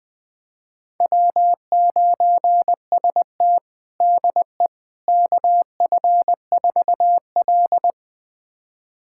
Morse code, heard as a keyed tone, about 20 wpm, 700 Hz. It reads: W9ST DE KF4L
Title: USA Callsign DE Callsign - 4 Characters - 20wpm